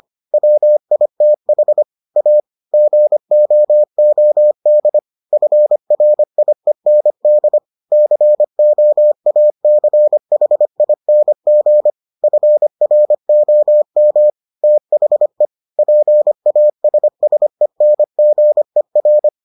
Morse code, heard as a keyed tone, about 25 wpm, 600 Hz. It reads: WITH A GOOD FRIEND COACHING FROM THE PASSENGER